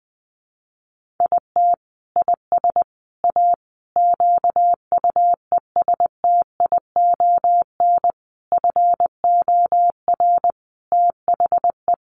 Morse code, heard as a keyed tone, about 20 wpm, 700 Hz. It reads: IT IS A QUESTION FOR THE
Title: Sentences from Top 1300 Words 20wpm